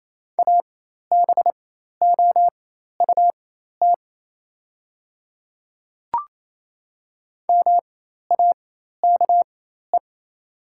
Morse code, heard as a keyed tone, about 28 wpm, spaced as for 13 wpm, 700 Hz. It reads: ABOUT MAKE
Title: 4x Character Spacing - Top 100 Words - 28wpm